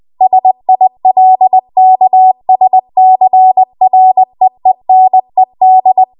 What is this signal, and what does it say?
Morse code, heard as a keyed tone, about 20 wpm, 750 Hz. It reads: SILKSCREENED
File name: Archive / 20 WPM / 20WPM